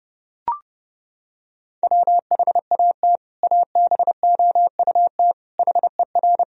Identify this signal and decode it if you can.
Morse code, heard as a keyed tone, about 30 wpm, 700 Hz. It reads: WHAT ABOUT HER
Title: Sentences from Top 1000 Words - Review Part 1 - 30wpm